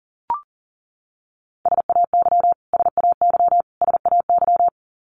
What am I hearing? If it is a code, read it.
Morse code, heard as a keyed tone, about 40 wpm, 700 Hz. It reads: SAY SAY SAY
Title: Top 500 Words Repeated 40wpm